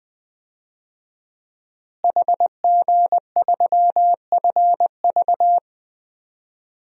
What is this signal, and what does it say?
Morse code, heard as a keyed tone, about 21 wpm, 700 Hz. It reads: HG3FV
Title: DXCC-100 Call Signs - 5 Characters - Repeated 3x - 20wpm